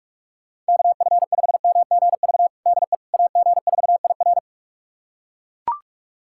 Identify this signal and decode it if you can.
Morse code, heard as a keyed tone, about 42 wpm, 700 Hz. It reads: KF5MGV DE AG4IR
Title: USA Callsign DE Callsign 45wpm